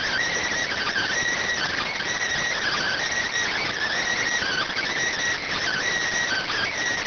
Emergency Alert System (SAME) tones (header bursts), heard as harsh buzzing bursts. It is heard from 0.7 to 6.7 s.